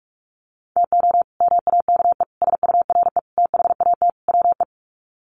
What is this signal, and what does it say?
Morse code, heard as a keyed tone, about 44 wpm, 700 Hz. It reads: TO MAKE SURE THAT WE
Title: Sets of 5 Words Encore 45wpm